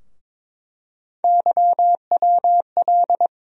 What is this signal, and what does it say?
Morse code, heard as a keyed tone, about 21 wpm, 700 Hz. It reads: YWL
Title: USA Call Sign Postfixes - 3 Letters 22wpm